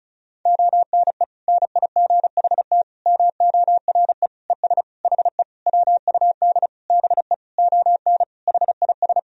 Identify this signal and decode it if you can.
Morse code, heard as a keyed tone, about 36 wpm, 700 Hz. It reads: ONE NIGHT MORE ES HE WUD BE ON HIS
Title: Sentences from Top 300 Words and Abbreviations 35wpm